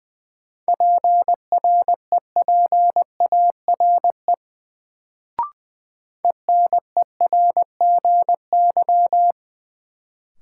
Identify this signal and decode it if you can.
Morse code, heard as a keyed tone, about 20 wpm, 700 Hz. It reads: PREPARE ENERGY
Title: Sentences from Top 900 Words - New Word Drill - 20wpm